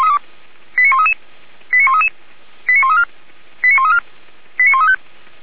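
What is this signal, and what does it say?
Five-tone selective call (ZVEI), heard as quick runs of five tones, tones of about 70 ms. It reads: 89133 89133 89134 89134 89135
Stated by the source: ZVEI five-tone selective call received on an SDR.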